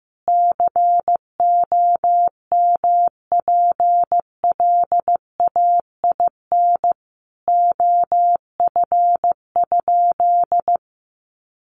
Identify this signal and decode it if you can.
Morse code, heard as a keyed tone, about 15 wpm, 700 Hz. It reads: COMPLAIN OF?
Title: Sentences from Top 1300 Words 15wpm